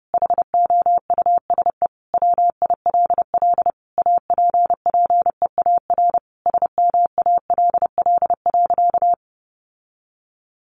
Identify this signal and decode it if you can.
Morse code, heard as a keyed tone, about 30 wpm, 700 Hz. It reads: HOUSE WILL APPEAR SMALL.
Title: Sentences from Top 1500 Words 30wpm